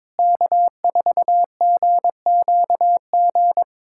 Morse code, heard as a keyed tone, about 21 wpm, 700 Hz. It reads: K4GQG
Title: USA Callsign DE Callsign - 5 Characters - 22wpm